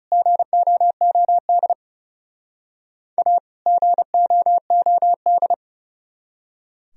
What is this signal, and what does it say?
Morse code, heard as a keyed tone, about 37 wpm, 700 Hz. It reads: GOOD A GOOD
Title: Speed-Racing - Sets of 2 Words 30wpm